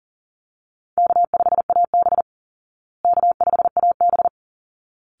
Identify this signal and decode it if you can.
Morse code, heard as a keyed tone, about 40 wpm, 700 Hz. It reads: K5AB K5AB
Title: Easy Call Signs Repeated 2x - 40wpm